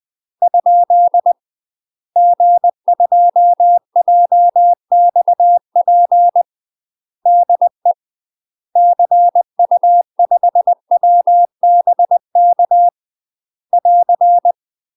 Morse code, heard as a keyed tone, about 20 wpm, 700 Hz. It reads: ? G2JXP DE CU5WBK <AR>
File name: UK MASTER QSOs TRAINING / 20WPM / UK MASTER QSOs 0038